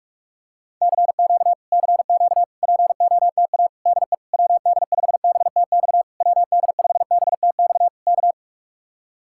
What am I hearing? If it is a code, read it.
Morse code, heard as a keyed tone, about 47 wpm, 700 Hz. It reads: CQ CQ POTA DE WD5BTX WD5BTX K